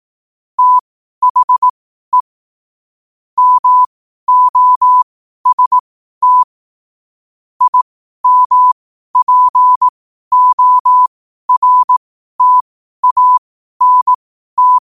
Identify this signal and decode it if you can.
Morse code, heard as a keyed tone, about 18 wpm, spaced as for 12 wpm, 1000 Hz. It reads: THE MOST IMPORTANT
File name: qotdfiles / 12wpm